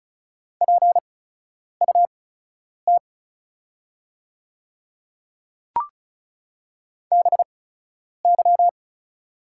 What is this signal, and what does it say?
Morse code, heard as a keyed tone, about 36 wpm, spaced as for 10 wpm, 700 Hz. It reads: PUT BY
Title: 8x Character Spacing - Top 200 Words - 35wpm